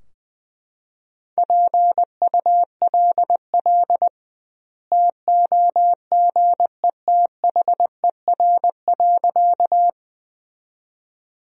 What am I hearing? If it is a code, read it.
Morse code, heard as a keyed tone, about 21 wpm, 700 Hz. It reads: PULL TOGETHER.
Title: 2x Word-Spacing - Sentences from Top 400 Words - 20wpm